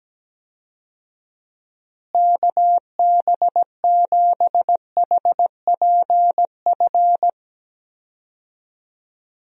Morse code, heard as a keyed tone, about 17 wpm, 700 Hz. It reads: KB7HPF
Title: USA Call Signs - 6 Characters - Repeated 2x - 17wpm